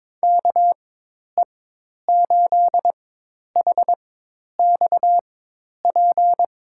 Morse code, heard as a keyed tone, about 22 wpm, spaced as for 10 wpm, 700 Hz. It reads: KE8HXP
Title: USA Call Signs - 4x Character Spacing - 22wpm